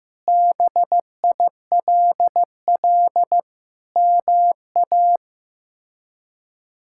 Morse code, heard as a keyed tone, about 15 wpm, 700 Hz. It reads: BILL MA
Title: K1USN SST QSO Practice - 15wpm